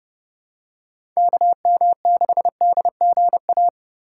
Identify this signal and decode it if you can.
Morse code, heard as a keyed tone, about 30 wpm, 700 Hz. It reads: KM6DGA
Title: USA Call Signs - 6 Characters - 2x Character Spacing - 30wpm